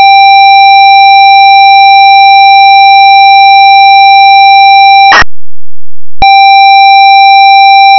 ZX Spectrum tape loading.
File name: zx ringtone